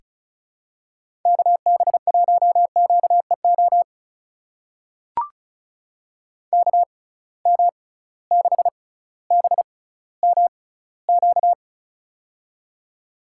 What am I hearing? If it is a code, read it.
Morse code, heard as a keyed tone, about 34 wpm, spaced as for 18 wpm, 700 Hz. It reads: KB1QEO KM6BMQ
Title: USA Call Signs - 6 Characters - 6x Character Spacing - 35wpm